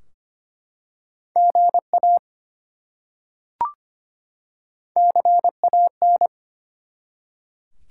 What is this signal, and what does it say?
Morse code, heard as a keyed tone, about 25 wpm, 700 Hz. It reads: GA CAN
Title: Top 90 QSO Elements 25wpm